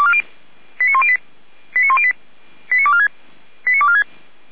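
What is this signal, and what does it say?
Five-tone selective call (ZVEI), heard as quick runs of five tones, tones of about 70 ms. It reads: 89108 89108 89246 89246
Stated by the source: ZVEI five-tone call received by SDR, archived by Feuerwehr Orlamünde (fire brigade).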